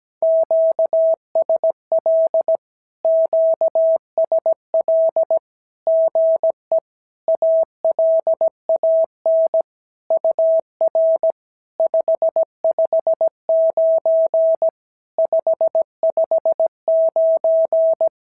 Morse code, heard as a keyed tone, about 17 wpm, 650 Hz. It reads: QSL QSL GE ALAN UR 559 559
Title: SOTA QSO Practice 17wpm